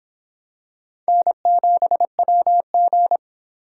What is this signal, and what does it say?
Morse code, heard as a keyed tone, about 24 wpm, 700 Hz. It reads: N7WG